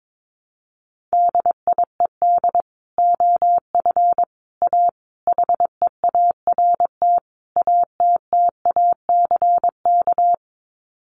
Morse code, heard as a keyed tone, about 22 wpm, 700 Hz. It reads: DIED OF A HEART ATTACK
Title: Sets of 5 Words Encore2 22wpm